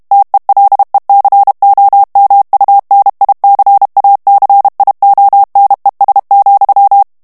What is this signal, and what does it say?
Morse code, heard as a keyed tone, about 32 wpm, 800 Hz. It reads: TELECOMUNICACIONES,
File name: ESPAGNOL TEXTE EN CLAIR 2 VITESSE 30 MOTS MINUTES